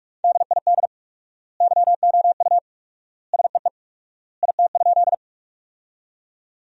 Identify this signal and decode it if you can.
Morse code, heard as a keyed tone, about 44 wpm, 700 Hz. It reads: DID YOU SEE IT?
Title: 4x Word-Spacing - Sentences from Top 100 Words - 45wpm